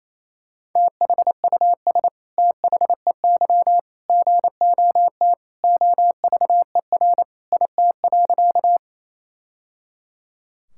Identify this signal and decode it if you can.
Morse code, heard as a keyed tone, about 28 wpm, 700 Hz. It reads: THUS THEY GOT OVER IT.
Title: Sentences from Top 1000 Words - Review Part 5 - 28wpm